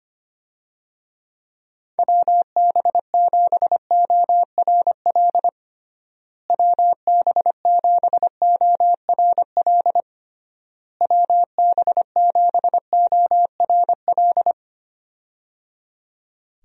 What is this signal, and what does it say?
Morse code, heard as a keyed tone, about 25 wpm, 700 Hz. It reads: WB7ORL WB7ORL WB7ORL
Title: USA Call Signs - 6 Characters - Repeated 3x - 25wpm